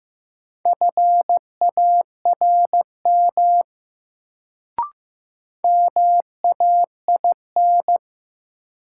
Morse code, heard as a keyed tone, about 15 wpm, 700 Hz. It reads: FARM MAIN
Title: Sentences from Top 1000 Words - Review Part 2 - New Word Drill - 15wpm